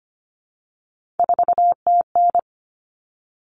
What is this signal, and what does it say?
Morse code, heard as a keyed tone, about 25 wpm, 700 Hz. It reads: 4TN
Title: All Cut Numbers - 3-Digit 25wpm